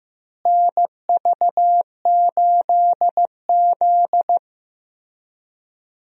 Morse code, heard as a keyed tone, about 15 wpm, 700 Hz. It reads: NV8Z
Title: USA Call Signs - 4 Characters - Repeated 3x - 15wpm